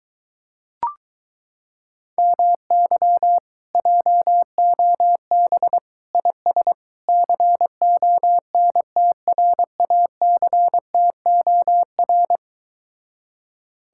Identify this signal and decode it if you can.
Morse code, heard as a keyed tone, about 23 wpm, 700 Hz. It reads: MY JOB IS CONTRACTOR